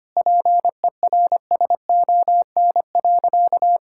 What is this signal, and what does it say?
Morse code, heard as a keyed tone, about 25 wpm, 700 Hz. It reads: PERSON.
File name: practice / 25wpm / Sentences with 3-Word Phrases - 25wpm